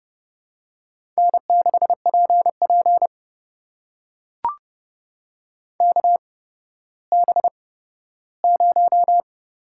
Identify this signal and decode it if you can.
Morse code, heard as a keyed tone, about 30 wpm, spaced as for 14 wpm, 700 Hz. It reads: N6PP KB0
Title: USA Call Signs - 8x Character Spacing - 30wpm